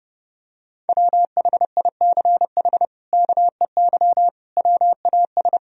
Morse code, heard as a keyed tone, about 30 wpm, 700 Hz. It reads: WHICH KEY WAS
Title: Sentences from Top 1000 Words - Review Part 4 - 30wpm